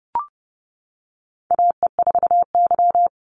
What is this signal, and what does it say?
Morse code, heard as a keyed tone, about 30 wpm, 700 Hz. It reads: AE4Y
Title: NA SOTA Chaser Callsigns 30wpm